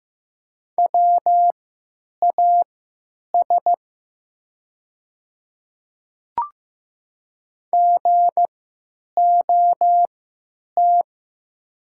Morse code, heard as a keyed tone, about 15 wpm, spaced as for 9 wpm, 700 Hz. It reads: WAS GOT
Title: 3x Character Spacing - Top 500 Words - 15wpm